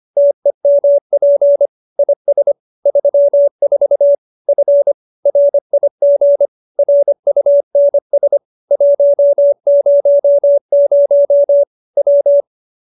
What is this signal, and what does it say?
Morse code, heard as a keyed tone, about 25 wpm, 570 Hz. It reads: TEMP IS 34 F RIG RUNS 100 W
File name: qso207b 25